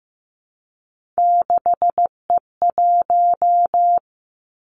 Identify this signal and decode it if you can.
Morse code, heard as a keyed tone, about 15 wpm, 700 Hz. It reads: 6E1
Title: Three Challenging Letter-Number Combinations 15wpm